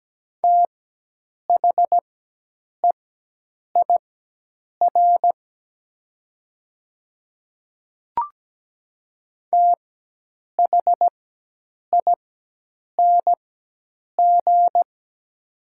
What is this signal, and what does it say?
Morse code, heard as a keyed tone, about 17 wpm, spaced as for 8 wpm, 700 Hz. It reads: THEIR THING
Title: 4x Character Spacing - Top 100 Words - 17wpm